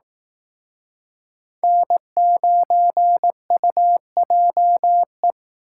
Morse code, heard as a keyed tone, about 18 wpm, 700 Hz. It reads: N9UJE
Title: Speed Racing - USA 1x3 Call Signs 15wpm